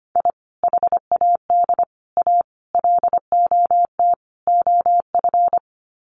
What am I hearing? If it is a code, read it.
Morse code, heard as a keyed tone, about 25 wpm, 700 Hz. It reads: I HAD A LOT OF
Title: Sets of 5 Words Encore 25wpm